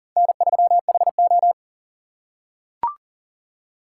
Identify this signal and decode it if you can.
Morse code, heard as a keyed tone, about 37 wpm, 700 Hz. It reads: N3HO
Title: USA Call Signs - 3x Character Spacing - 40wpm